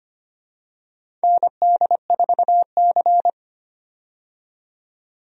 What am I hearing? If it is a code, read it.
Morse code, heard as a keyed tone, about 25 wpm, 700 Hz. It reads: ND4C
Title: USA Call Signs - 4 Characters - 25wpm